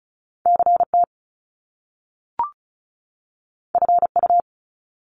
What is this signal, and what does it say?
Morse code, heard as a keyed tone, about 36 wpm, 700 Hz. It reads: CT FU
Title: Top Two Letter Combinations in Words 35wpm